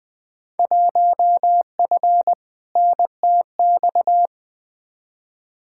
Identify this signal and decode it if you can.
Morse code, heard as a keyed tone, about 20 wpm, 700 Hz. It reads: 1F NTX
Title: Field Day Exchanges Encore - 20wpm